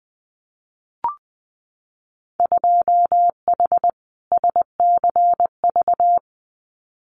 Morse code, heard as a keyed tone, about 20 wpm, 700 Hz. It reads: 2H SCV
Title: Winter Field Day Exchanges Encore 20wpm